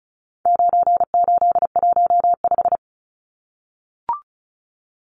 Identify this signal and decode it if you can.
Morse code, heard as a keyed tone, about 34 wpm, 700 Hz. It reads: 9815
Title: Numbers - 4-Digit 35wpm